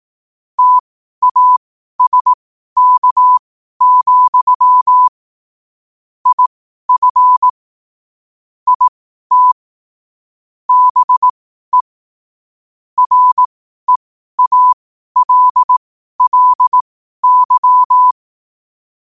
Morse code, heard as a keyed tone, about 18 wpm, spaced as for 12 wpm, 1000 Hz. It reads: TASK, IF IT BE REALLY
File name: qotdfiles / 12wpm